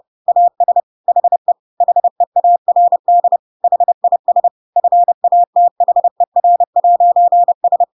Morse code, heard as a keyed tone, about 30 wpm, 700 Hz. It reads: AS HE HEARD HIS FATHER'S
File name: Pinocchio / Pinocchio ch7 30 30 ik1hge